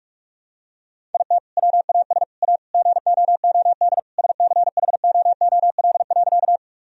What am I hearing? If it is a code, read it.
Morse code, heard as a keyed tone, about 45 wpm, 700 Hz. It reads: IT WAS A GOOD SCHOOL.
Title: Sentences from Top 1000 Words - Review Part 1 - 45wpm